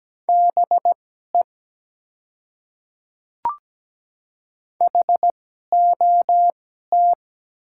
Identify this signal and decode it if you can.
Morse code, heard as a keyed tone, about 17 wpm, spaced as for 12 wpm, 690 Hz. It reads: BE HOT
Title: 2x Character Spacing - Top 100 Words - 17wpm